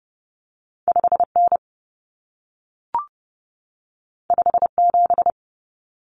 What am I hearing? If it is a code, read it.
Morse code, heard as a keyed tone, about 30 wpm, 700 Hz. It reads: 5N 57
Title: Common Cut Numbers - 2-Digit 30wpm